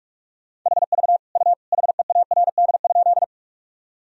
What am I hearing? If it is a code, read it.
Morse code, heard as a keyed tone, about 46 wpm, 700 Hz. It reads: HV U HEARD?
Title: Sentences from Top 400 Words and Abbreviations 45wpm